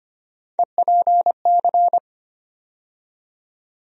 Morse code, heard as a keyed tone, about 25 wpm, 700 Hz. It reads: EPC
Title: USA Call Sign Postfixes - 3 Letters 25wpm